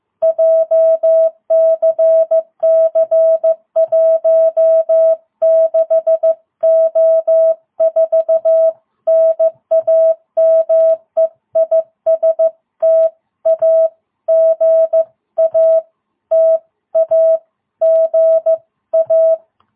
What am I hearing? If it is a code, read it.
Morse code, heard as a keyed tone, about 14 wpm, 660 Hz. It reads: JCC16O4 NAME IS TAGA TAGA
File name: jh8cbh / 2022cw